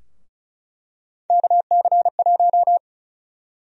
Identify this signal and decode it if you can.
Morse code, heard as a keyed tone, about 36 wpm, 700 Hz. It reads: KC1
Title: USA Call Sign Prefixes - Repeated 2x - 35wpm